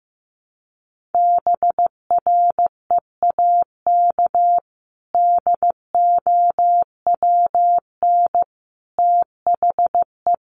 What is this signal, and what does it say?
Morse code, heard as a keyed tone, about 15 wpm, 700 Hz. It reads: BREAK DOWN THE
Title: Sentences from Top 900 Words 15wpm